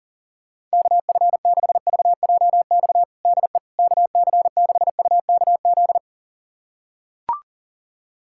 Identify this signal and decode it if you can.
Morse code, heard as a keyed tone, about 40 wpm, 700 Hz. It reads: KF6VJX DE KC6UKZ
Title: USA Callsign DE Callsign - 6 Characters - 40wpm